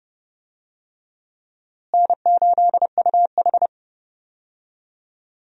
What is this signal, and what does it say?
Morse code, heard as a keyed tone, about 31 wpm, 700 Hz. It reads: N8UH
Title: USA Call Signs - 4 Characters - Repeated 2x - 30wpm